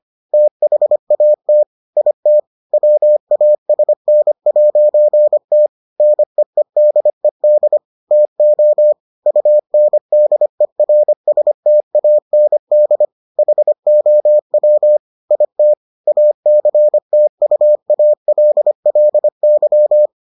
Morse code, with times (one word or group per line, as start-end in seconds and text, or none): 0.3-1.6 THAT
2.0-2.4 IT
2.7-5.7 WASN'T
6.0-7.8 NEEDED
8.1-8.9 TO
9.3-13.1 UNDERSTAND
13.4-15.0 HOW
15.3-15.7 IT
16.1-20.1 ACTUALLY